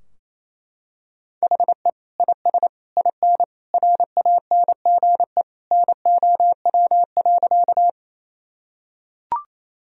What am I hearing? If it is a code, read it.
Morse code, heard as a keyed tone, about 28 wpm, 700 Hz. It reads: HE IS IN RANGE NOW.
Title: Sentences from Top 1000 Words - Review Part 4 - 28wpm